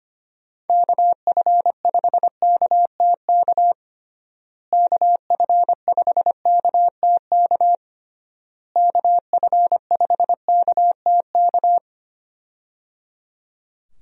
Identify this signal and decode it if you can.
Morse code, heard as a keyed tone, about 25 wpm, 700 Hz. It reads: KF5KTK KF5KTK KF5KTK
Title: USA Call Signs - 6 Characters - Repeated 3x - 25wpm